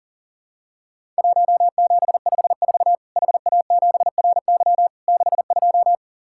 Morse code, heard as a keyed tone, about 44 wpm, 700 Hz. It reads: 1754 HA7RY 62
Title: Worked All Europe DX Contest - QTC Exchange - 40wpm